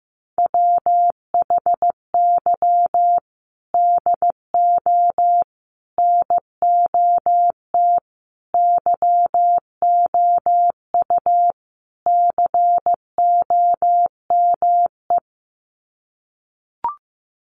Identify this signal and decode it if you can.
Morse code, heard as a keyed tone, about 15 wpm, 700 Hz. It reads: WHY DO NOT YOU COME
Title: Sets of 5 Words Encore2 15wpm